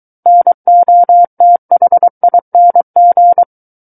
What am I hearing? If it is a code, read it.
Morse code, heard as a keyed tone, about 23 wpm, 700 Hz. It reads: NOTHING
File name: Book One / 14-Chapter 14 - 23 WPM